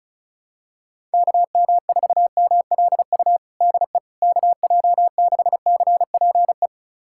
Morse code, heard as a keyed tone, about 36 wpm, 700 Hz. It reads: KM4MLU DE KJ6CPE
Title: USA Callsign DE Callsign 35wpm